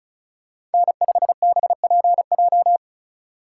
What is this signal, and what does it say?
Morse code, heard as a keyed tone, about 36 wpm, 700 Hz. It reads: N5BPJ
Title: USA Call Signs - 2x Character Spacing - 35wpm